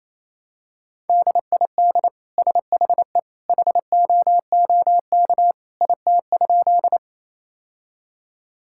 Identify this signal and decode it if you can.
Morse code, heard as a keyed tone, about 28 wpm, 700 Hz. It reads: DID SHE HOOK IT?
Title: Sentences from Top 1600 Words 28wpm